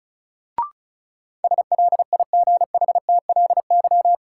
Morse code, heard as a keyed tone, about 36 wpm, 700 Hz. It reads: SLIGHTLY
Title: Sentences from Top 1300 Words - New Word Drill - 35wpm